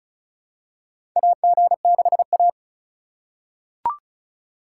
Morse code, heard as a keyed tone, about 36 wpm, 700 Hz. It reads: AG6A
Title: USA Call Signs - 4 Characters - 4x Character Spacing - 35wpm